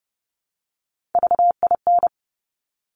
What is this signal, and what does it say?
Morse code, heard as a keyed tone, about 30 wpm, 700 Hz. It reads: VIN
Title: Top Three Letter Combinations in Words 30wpm